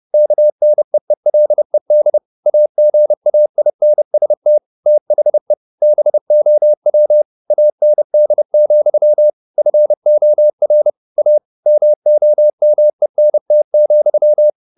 Morse code, heard as a keyed tone, about 30 wpm, 600 Hz. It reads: KNEELED AGAINST THE BOW AND, FOR A MOMENT,